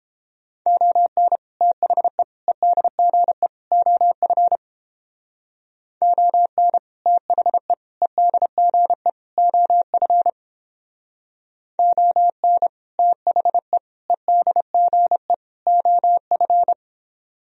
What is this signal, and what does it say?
Morse code, heard as a keyed tone, about 29 wpm, 700 Hz. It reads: ON THE EDGE OF ON THE EDGE OF ON THE EDGE OF